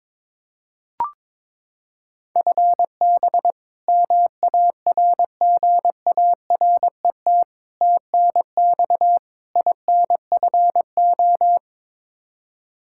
Morse code, heard as a keyed tone, about 22 wpm, 700 Hz. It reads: FB MARGARET TNX INFO
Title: Practice QSO - Exchange 3 - More Info - 22wpm